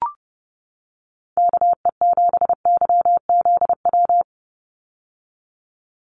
Morse code, heard as a keyed tone, about 30 wpm, 700 Hz. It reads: KE7YZW
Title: USA Call Signs 30wpm